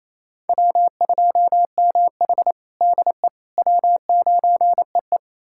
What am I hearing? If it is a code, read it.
Morse code, heard as a keyed tone, about 27 wpm, 700 Hz. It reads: W2MH DE W9EE
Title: USA Callsign DE Callsign - 4 Characters - 28wpm